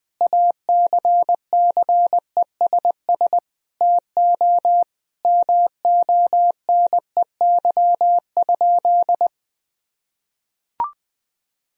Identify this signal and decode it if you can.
Morse code, heard as a keyed tone, about 20 wpm, 700 Hz. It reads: ACCESS TO MONEY?
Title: Sentences from Top 1000 Words - Review Part 4 - 20wpm